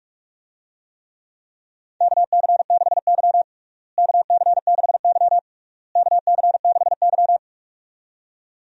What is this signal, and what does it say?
Morse code, heard as a keyed tone, about 46 wpm, 700 Hz. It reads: KC6Y KC6Y KC6Y
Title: USA Call Signs - 4 Characters - Repeated 3x - 45wpm